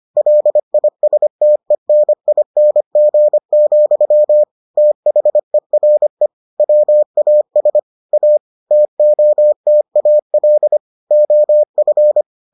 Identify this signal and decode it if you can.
Morse code, heard as a keyed tone, about 25 wpm, 600 Hz. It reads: LISTENING, THERE WAS A TOTAL OF